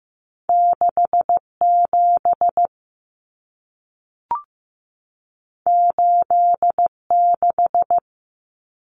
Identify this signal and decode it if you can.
Morse code, heard as a keyed tone, about 15 wpm, 700 Hz. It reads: 6 7 8 6
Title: Numbers - 2-Digit 15wpm